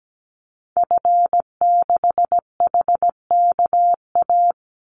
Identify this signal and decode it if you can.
Morse code, heard as a keyed tone, about 17 wpm, 700 Hz. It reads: F6HKA
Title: Call Signs Repeated 2x - 17wpm